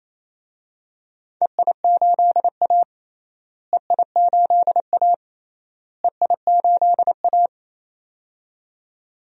Morse code, heard as a keyed tone, about 27 wpm, 700 Hz. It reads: EI8A EI8A EI8A
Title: DXCC-100 Call Signs - 4 Characters - Repeated 3x - 28wpm